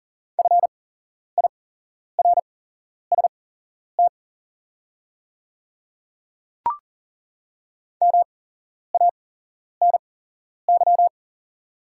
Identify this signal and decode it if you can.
Morse code, heard as a keyed tone, about 38 wpm, spaced as for 11 wpm, 690 Hz. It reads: FIRST MANY
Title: 8x Character Spacing - Top 100 Words - 40wpm